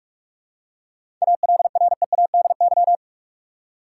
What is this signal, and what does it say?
Morse code, heard as a keyed tone, about 46 wpm, 700 Hz. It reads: ALREADY
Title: Seven Letter Words - 45wpm